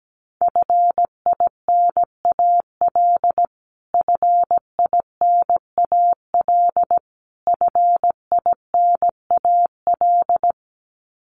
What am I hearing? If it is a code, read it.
Morse code, heard as a keyed tone, about 17 wpm, 700 Hz. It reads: FINAL FINAL FINAL
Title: Top 500 Words Repeated 17wpm